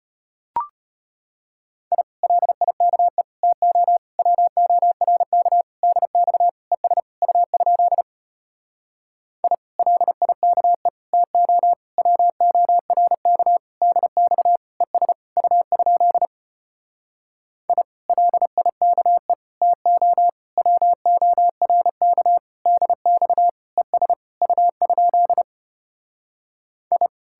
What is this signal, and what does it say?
Morse code, heard as a keyed tone, about 33 wpm, 700 Hz. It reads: I LIKE TO WORK DX ES U? I LIKE TO WORK DX ES U? I LIKE TO WORK DX ES U? I